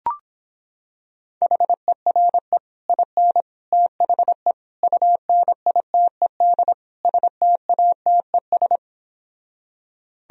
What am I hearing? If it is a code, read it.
Morse code, heard as a keyed tone, about 26 wpm, 700 Hz. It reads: HERE IN THE UNITED STATES